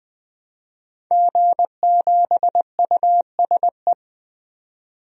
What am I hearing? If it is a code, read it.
Morse code, heard as a keyed tone, about 21 wpm, 700 Hz. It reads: G7USE
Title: DXCC-100 Call Signs - 5 Characters - 20wpm